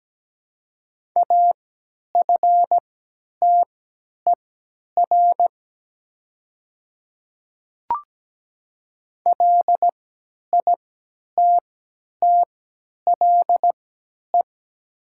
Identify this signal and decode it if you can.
Morse code, heard as a keyed tone, about 17 wpm, spaced as for 10 wpm, 700 Hz. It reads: AFTER LITTLE
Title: 3x Character Spacing - Top 300 Words - 17wpm